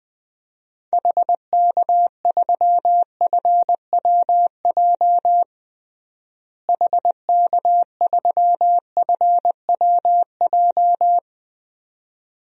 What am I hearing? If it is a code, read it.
Morse code, heard as a keyed tone, about 21 wpm, 700 Hz. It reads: HK3FWJ HK3FWJ
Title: DXCC-100 Call Signs - 6 Characters - Repeated 2x - 20wpm